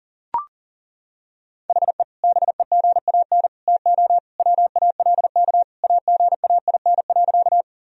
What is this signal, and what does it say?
Morse code, heard as a keyed tone, about 37 wpm, 700 Hz. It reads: HE BEGAN TO WALK AGAIN.
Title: Sentences from Top 1000 Words - Review Part 1 - 40wpm